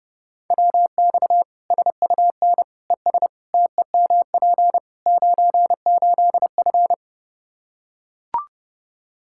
Morse code, heard as a keyed tone, about 30 wpm, 700 Hz. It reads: WX SUN ES TEMP 98F
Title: K1USN SST QSO Practice 30wpm